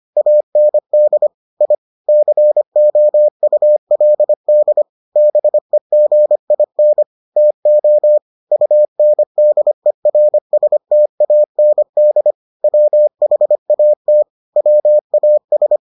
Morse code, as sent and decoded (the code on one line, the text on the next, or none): .- -. -.. / .. / -.-. --- ..- .-.. -.. / -... --. .. -. / - --- / ..- -. -.. .-. ... - .- -. -.. / .-- .... .- - / .-- .- ...
AND I COULD BEGIN TO UNDERSTAND WHAT WAS